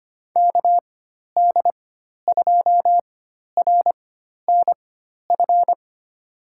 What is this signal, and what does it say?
Morse code, heard as a keyed tone, about 25 wpm, spaced as for 12 wpm, 700 Hz. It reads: KD2RNF
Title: USA Call Signs - 4x Character Spacing - 25wpm